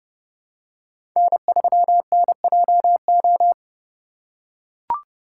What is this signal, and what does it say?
Morse code, heard as a keyed tone, about 29 wpm, 700 Hz. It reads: N3NJO
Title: USA Call Signs - 5 Characters - 3x Character Spacing - 30wpm